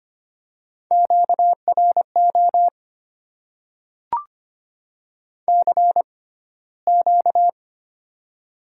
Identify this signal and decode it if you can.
Morse code, heard as a keyed tone, about 25 wpm, spaced as for 15 wpm, 700 Hz. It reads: QRO CQ
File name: Top 90 QSO Elements - 6x Character Spacing - 25wpm